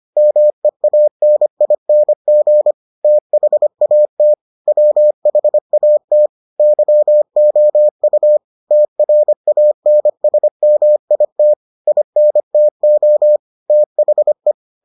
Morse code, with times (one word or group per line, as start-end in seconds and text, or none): 0.2-2.7 MEANING
3.0-4.3 THAT
4.7-6.3 WHAT
6.6-8.4 YOU
8.7-11.5 TRANSMIT
11.9-13.4 INTO
13.7-14.5 THE